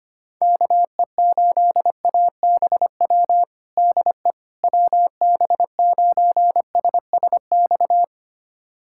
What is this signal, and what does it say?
Morse code, heard as a keyed tone, about 25 wpm, 700 Hz. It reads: KE8ABW DE WB9SSX
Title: USA Callsign DE Callsign - 6 Characters - 25wpm